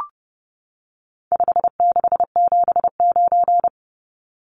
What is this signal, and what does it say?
Morse code, heard as a keyed tone, about 30 wpm, 700 Hz. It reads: 5679
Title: Numbers - 4-Digit 30wpm